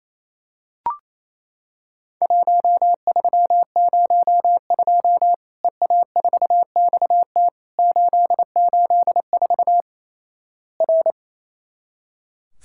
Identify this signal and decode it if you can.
Morse code, heard as a keyed tone, about 28 wpm, 700 Hz. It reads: 1302 EA4XT 884 R
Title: Worked All Europe DX Contest - QTC Exchange - 28wpm